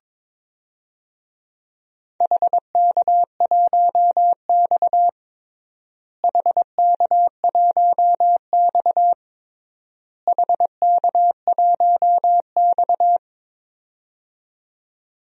Morse code, heard as a keyed tone, about 22 wpm, 700 Hz. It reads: HK1X HK1X HK1X
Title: DXCC-100 Call Signs - 4 Characters - Repeated 3x - 22wpm